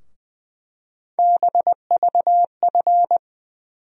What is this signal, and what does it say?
Morse code, heard as a keyed tone, about 21 wpm, 700 Hz. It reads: BVF
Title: USA Call Sign Postfixes - 3 Letters 20wpm